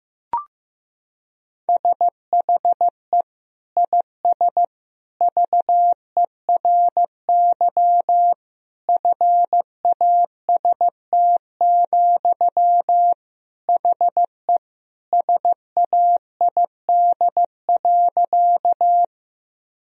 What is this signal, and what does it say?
Morse code, heard as a keyed tone, about 15 wpm, 700 Hz. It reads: SHE IS VERY FAST, HE SAID.
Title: Sentences from Top 1000 Words - Review Part 2 - 15wpm